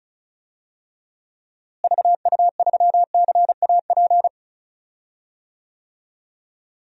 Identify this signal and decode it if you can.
Morse code, heard as a keyed tone, about 36 wpm, 700 Hz. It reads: VU3CAP
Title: DXCC-100 Call Signs - 6 Characters - Repeated 3x - 35wpm